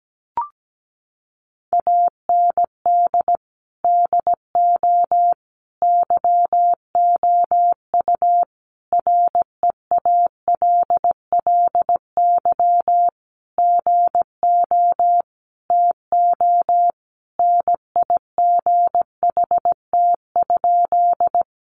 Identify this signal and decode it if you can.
Morse code, heard as a keyed tone, about 17 wpm, 700 Hz. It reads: AND DO YOU REALLY GO TO NIGHT?
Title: Sentences from Top 600 Words 17wpm